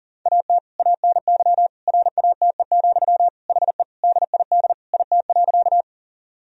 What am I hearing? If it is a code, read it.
Morse code, heard as a keyed tone, about 40 wpm, 690 Hz. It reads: AT ANY RATE, HE DID IT.
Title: Sentences from Top 1000 Words - Review Part 4 - 40wpm